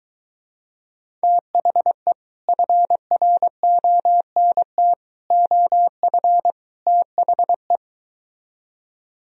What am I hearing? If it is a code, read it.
Morse code, heard as a keyed tone, about 23 wpm, 700 Hz. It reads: THE FRONT OF THE